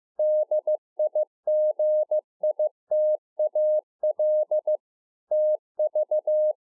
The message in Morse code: DIGITAL TV

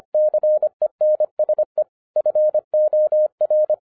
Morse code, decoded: CENSE FOR

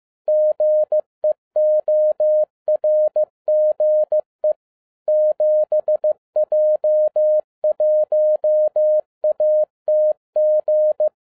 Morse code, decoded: GEORGE 7J1ATG

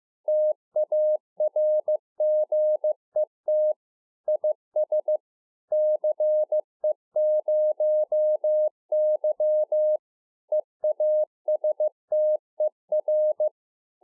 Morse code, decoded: TARGET IS CE0Y EASTER

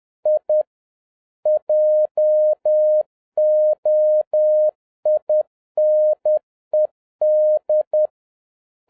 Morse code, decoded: I JOINED

